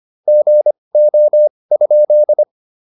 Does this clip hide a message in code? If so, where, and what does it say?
Yes: Morse code from 0.3 to 2.4 s, heard as a keyed tone, about 25 wpm, spaced as for 20 wpm, 600 Hz. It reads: GO?